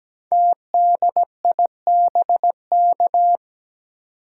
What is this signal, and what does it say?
Morse code, heard as a keyed tone, about 17 wpm, 700 Hz. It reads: TDIBK
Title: Code Groups - 5 Letters - 17wpm